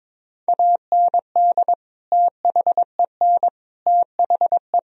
Morse code, heard as a keyed tone, about 22 wpm, 700 Hz. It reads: AND THEN THE
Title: Sentences from Top 1000 Words - Review Part 3 - 22wpm